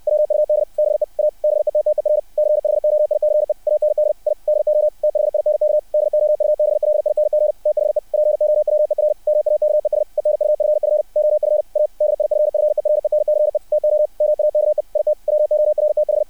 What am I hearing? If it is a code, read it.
Morse code, heard as a keyed tone, about 60 wpm, 600 Hz. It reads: WUD BE A BETTER CHOICE AND I DO THINK DOUBLING THE POWER MAKES ENUFF OF A DIFFERENCE TO MAKE IT WORTH
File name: archived / CRSnet